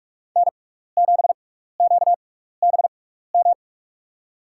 Morse code, heard as a keyed tone, about 44 wpm, spaced as for 15 wpm, 700 Hz. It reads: N7QBM
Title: USA Call Signs - 5 Characters - 6x Character Spacing - 45wpm